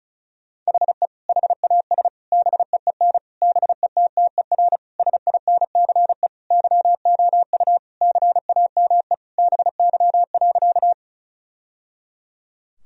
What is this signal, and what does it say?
Morse code, heard as a keyed tone, about 34 wpm, 700 Hz. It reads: HE HAS BEEN BETTER SINCE YOU CAME BY.
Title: Sentences from Top 1000 Words - Review Part 1 - 35wpm